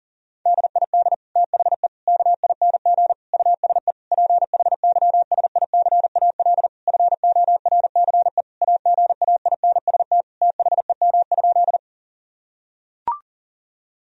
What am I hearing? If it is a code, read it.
Morse code, heard as a keyed tone, about 37 wpm, 700 Hz. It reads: DID THE KING USE PHYSICAL FORCE AGAINST THEM?
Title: Sentences from Top 1000 Words - Review Part 4 - 40wpm